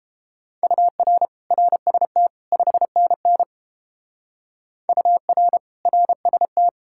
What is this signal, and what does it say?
Morse code, heard as a keyed tone, about 31 wpm, 700 Hz. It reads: UR RST 5NN UR RST